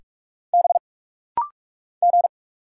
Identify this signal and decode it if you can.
Morse code, heard as a keyed tone, about 48 wpm, 700 Hz. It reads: B G
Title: Single Letter-Number Alt - Rapid-Fire 45wpm